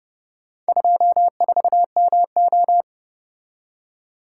Morse code, heard as a keyed tone, about 30 wpm, 700 Hz. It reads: 24MO